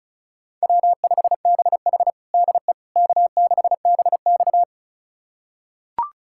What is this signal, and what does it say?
Morse code, heard as a keyed tone, about 34 wpm, 700 Hz. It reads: W5BH DE K6BX
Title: USA Callsign DE Callsign - 4 Characters - 35wpm